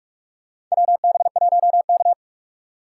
Morse code, heard as a keyed tone, about 46 wpm, 700 Hz. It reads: WB1K
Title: USA Call Signs - 4 Characters - 6x Character Spacing - 45wpm